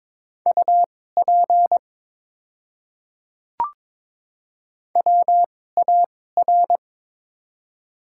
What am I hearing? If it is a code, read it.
Morse code, heard as a keyed tone, about 22 wpm, spaced as for 16 wpm, 700 Hz. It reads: UP WAR
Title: 2x Character Spacing - Top 500 Words - 22wpm